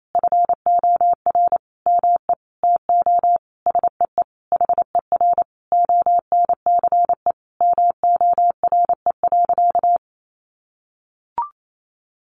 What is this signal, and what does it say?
Morse code, heard as a keyed tone, about 28 wpm, 700 Hz. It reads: FOR ME TO SEE HER ONCE MORE.
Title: Sentences from Top 500 Words 28wpm